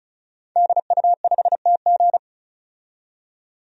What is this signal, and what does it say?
Morse code, heard as a keyed tone, about 36 wpm, 700 Hz. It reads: DU5TG
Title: DXCC-100 Call Signs - 5 Characters - 35wpm